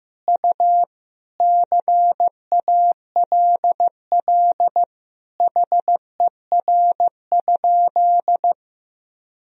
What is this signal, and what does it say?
Morse code, heard as a keyed tone, about 16 wpm, 700 Hz. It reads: U CALL HER?
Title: Sentences from Top 100 Words and Abbreviations 15wpm